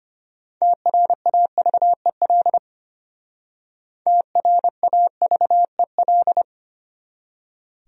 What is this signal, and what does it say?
Morse code, heard as a keyed tone, about 34 wpm, 700 Hz. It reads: TRAVEL TRAVEL